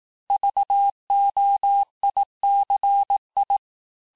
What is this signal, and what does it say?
Morse code, heard as a keyed tone, about 18 wpm, 800 Hz. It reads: VOICI
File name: cours-du-lundi files / cours-1852-texte-chiffre-vitesse-17MPM